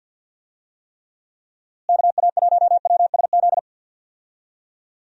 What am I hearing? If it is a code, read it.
Morse code, heard as a keyed tone, about 46 wpm, 700 Hz. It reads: KA1WSZ
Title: USA Call Signs - 6 Characters - Repeated 2x - 50wpm